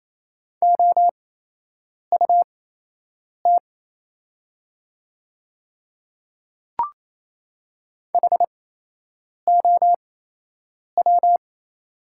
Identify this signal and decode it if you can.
Morse code, heard as a keyed tone, about 28 wpm, spaced as for 8 wpm, 700 Hz. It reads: OUT HOW
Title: 8x Character Spacing - Top 100 Words - 28wpm